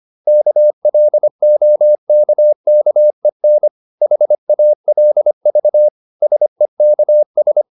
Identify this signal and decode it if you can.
Morse code, heard as a keyed tone, about 25 wpm, 600 Hz. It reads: KLOKKEN HALV SEKS
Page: Ferie 25 wpm